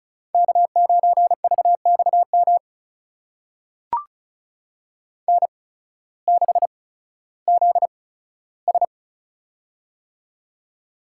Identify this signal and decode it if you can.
Morse code, heard as a keyed tone, about 36 wpm, spaced as for 16 wpm, 700 Hz. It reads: K9VXM N6ZS
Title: USA Call Signs - 8x Character Spacing - 35wpm